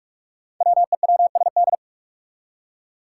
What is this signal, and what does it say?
Morse code, heard as a keyed tone, about 43 wpm, 700 Hz. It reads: WEWSD
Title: Five Challenging Letter-Number Combinations 45wpm